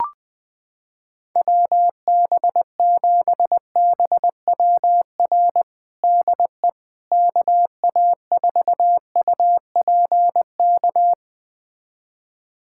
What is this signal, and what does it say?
Morse code, heard as a keyed tone, about 20 wpm, 700 Hz. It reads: WB7BWR DE KA4UPK
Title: USA Callsign DE Callsign 20wpm